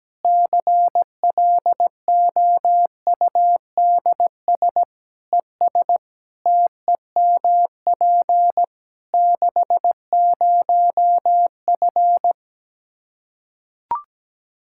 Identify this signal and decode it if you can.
Morse code, heard as a keyed tone, about 17 wpm, 700 Hz. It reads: CLOUDS ES TEMP 60F
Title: Practice QSO - Exchange 2 - Info - 17wpm